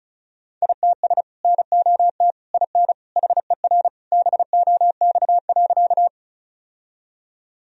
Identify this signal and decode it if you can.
Morse code, heard as a keyed tone, about 36 wpm, 700 Hz. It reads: ITS NOT IN HER BOX.
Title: Sentences from Top 1000 Words - Review Part 2 - 35wpm